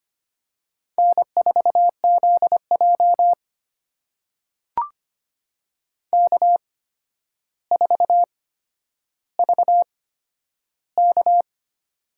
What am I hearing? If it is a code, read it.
Morse code, heard as a keyed tone, about 25 wpm, spaced as for 11 wpm, 700 Hz. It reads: N4ZJ K4VK
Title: USA Call Signs - 4 Characters - 8x Character Spacing - 25wpm